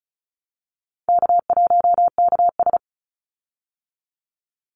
Morse code, heard as a keyed tone, about 34 wpm, 700 Hz. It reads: K1KS
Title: Easy Call Signs Repeated 2x - 35wpm